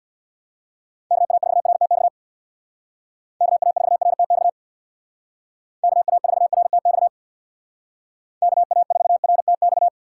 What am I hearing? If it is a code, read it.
Morse code, heard as a keyed tone, about 64 wpm, 700 Hz. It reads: KA4RTX KA4RTX KA4RTX KA4RTX